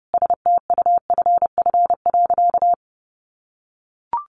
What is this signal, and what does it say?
Morse code, heard as a keyed tone, about 30 wpm, 700 Hz. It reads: STUFF.
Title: Sentences from Top 1500 Words 30wpm